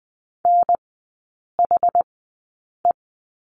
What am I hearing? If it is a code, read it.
Morse code, heard as a keyed tone, about 20 wpm, spaced as for 8 wpm, 700 Hz. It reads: NHE
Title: 060 Lesson 11e - Letter h - Three Characters 20wpm